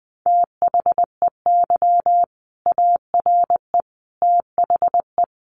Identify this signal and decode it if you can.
Morse code, heard as a keyed tone, about 20 wpm, 700 Hz. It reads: THEY ARE THE
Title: Sentences from Top 1200 Words 20wpm